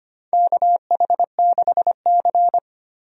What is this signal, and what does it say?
Morse code, heard as a keyed tone, about 25 wpm, 700 Hz. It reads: KH6C
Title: DXCC-100 Call Signs - 4 Characters - 25wpm